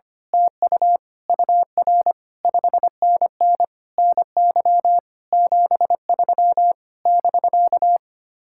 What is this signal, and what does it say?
Morse code, heard as a keyed tone, about 24 wpm, 700 Hz. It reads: TU UR 5NN NY 73 <BK>
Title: Speed-Racing Common QSO Phrases - 25wpm